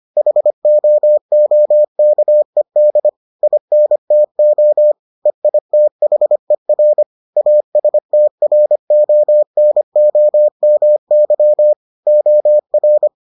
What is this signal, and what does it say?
Morse code, heard as a keyed tone, about 25 wpm, 600 Hz. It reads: HOOKED INTO EITHER ASTRONOMY OR